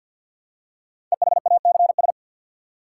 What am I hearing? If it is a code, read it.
Morse code, heard as a keyed tone, about 50 wpm, 700 Hz. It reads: EHACS
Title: Code Groups - 5 Letters - 50wpm